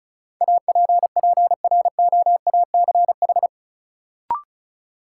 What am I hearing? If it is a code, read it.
Morse code, heard as a keyed tone, about 36 wpm, 700 Hz. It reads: APPROACH
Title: Sentences from Top 1400 Words - New Word Drill Alt - 35wpm